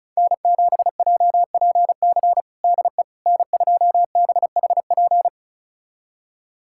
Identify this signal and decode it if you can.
Morse code, heard as a keyed tone, about 36 wpm, 700 Hz. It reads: N7JPC DE N2BHP
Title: USA Callsign DE Callsign - 5 Characters - 35wpm